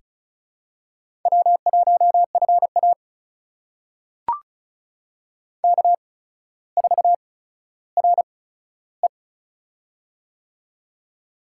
Morse code, heard as a keyed tone, about 36 wpm, spaced as for 15 wpm, 700 Hz. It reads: W1FA K4RE
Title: USA Call Signs - 4 Characters - 8x Character Spacing - 35wpm